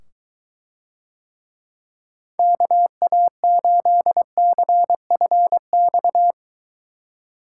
Morse code, heard as a keyed tone, about 22 wpm, 700 Hz. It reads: KA8CFX